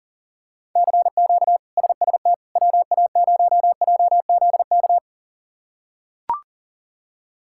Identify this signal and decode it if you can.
Morse code, heard as a keyed tone, about 37 wpm, 700 Hz. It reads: CQ SST WA0JZK
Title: K1USN SST QSO Practice - 40wpm